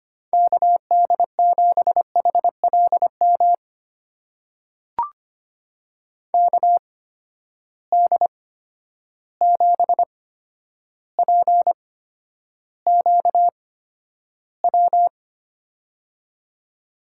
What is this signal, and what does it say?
Morse code, heard as a keyed tone, about 25 wpm, spaced as for 11 wpm, 700 Hz. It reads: KD7HLM KD7PQW
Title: USA Call Signs - 6 Characters - 8x Character Spacing - 25wpm